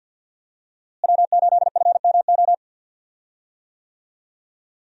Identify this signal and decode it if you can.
Morse code, heard as a keyed tone, about 50 wpm, 700 Hz. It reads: W8FMO
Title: USA Call Signs - 5 Characters - Repeated 2x - 50wpm